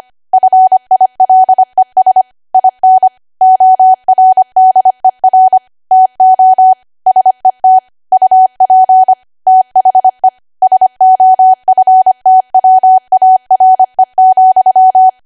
Morse code, heard as a keyed tone, about 25 wpm, 750 Hz. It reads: FILES IN ORDER TO SET UP THE SOFTWARE,
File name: Archive / 25 WPM / 25WPM